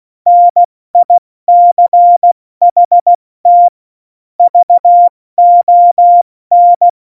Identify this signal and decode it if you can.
Morse code, heard as a keyed tone, about 16 wpm, spaced as for 14 wpm, 700 Hz. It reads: NICHT VON
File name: Morsekurs / Witze / Witz 25. Dezember, 70 F80